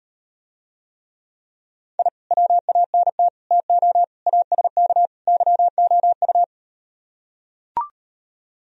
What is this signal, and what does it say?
Morse code, heard as a keyed tone, about 38 wpm, 700 Hz. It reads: I WANT TO ASK YOU